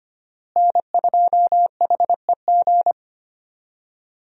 Morse code, heard as a keyed tone, about 25 wpm, 700 Hz. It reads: N2HEG
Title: USA Call Signs - 5 Characters - Repeated 3x - 25wpm